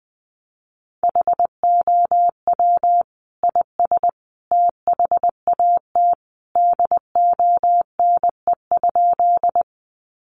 Morse code, heard as a keyed tone, about 20 wpm, 700 Hz. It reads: HOW IS THAT DONE?
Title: Sentences from Top 500 Words 20wpm